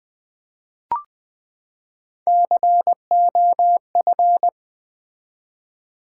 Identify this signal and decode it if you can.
Morse code, heard as a keyed tone, about 20 wpm, 700 Hz. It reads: COF
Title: Build A Word - 3 Letters At A Time - 6-Letter Words - 20wpm